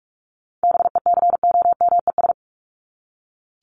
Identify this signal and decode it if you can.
Morse code, heard as a keyed tone, about 44 wpm, 700 Hz. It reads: BECOMES
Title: ES:S Suffix - Top 100 Words with Base Words 45wpm